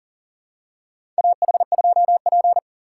Morse code, heard as a keyed tone, about 40 wpm, 700 Hz. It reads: AH2P
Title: USA Call Signs - 4 Characters - 3x Character Spacing - 40wpm